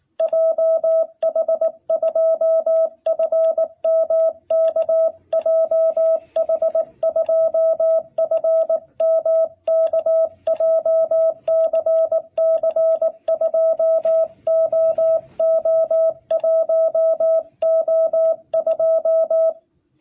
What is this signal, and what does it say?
Morse code, heard as a keyed tone, about 19 wpm, 660 Hz. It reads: JH2FMK JH2FMK JCC2OO1O2
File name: jh8cbh / 2021cw